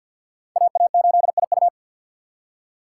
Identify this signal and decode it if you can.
Morse code, heard as a keyed tone, about 52 wpm, 700 Hz. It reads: AA8IU
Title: USA Call Signs - 2x Character Spacing - 50wpm